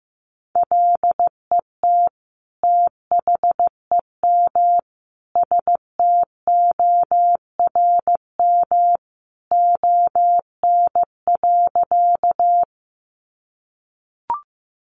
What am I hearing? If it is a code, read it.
Morse code, heard as a keyed tone, about 15 wpm, 700 Hz. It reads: LET THEM STORM ON.
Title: Sentences from Top 1500 Words 15wpm